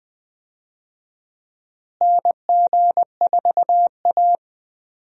Morse code, heard as a keyed tone, about 20 wpm, 700 Hz. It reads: NG4A
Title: USA Call Signs - 4 Characters - Repeated 3x - 20wpm